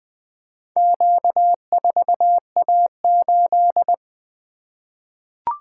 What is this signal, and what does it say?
Morse code, heard as a keyed tone, about 20 wpm, 700 Hz. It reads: Q4A8
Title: Four Letter-Number Combinations - 20wpm